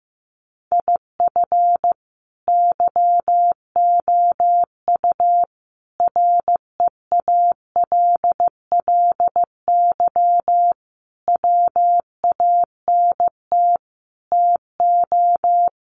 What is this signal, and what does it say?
Morse code, heard as a keyed tone, about 15 wpm, 700 Hz. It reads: IF YOU REALLY WANT TO
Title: Sets of 5 Words Encore2 15wpm